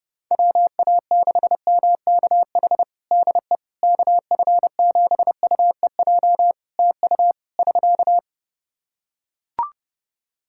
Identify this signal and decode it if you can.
Morse code, heard as a keyed tone, about 32 wpm, 700 Hz. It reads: WA6MKH DE KF7UEJ TU <SK>
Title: K1USN SST QSO Practice 30wpm